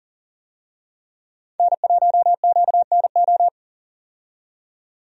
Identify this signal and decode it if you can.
Morse code, heard as a keyed tone, about 40 wpm, 700 Hz. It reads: N1QNO
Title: USA Call Signs - 5 Characters - 40wpm